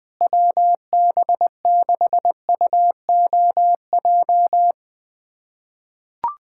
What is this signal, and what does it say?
Morse code, heard as a keyed tone, about 20 wpm, 700 Hz. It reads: WB6UOJ
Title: USA Callsign DE Callsign - 6 Characters - 20wpm